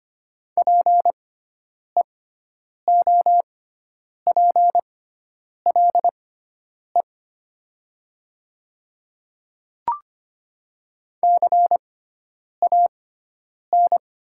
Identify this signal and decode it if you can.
Morse code, heard as a keyed tone, about 25 wpm, spaced as for 9 wpm, 700 Hz. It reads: PEOPLE CAN
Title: 6x Character Spacing - Top 100 Words - 25wpm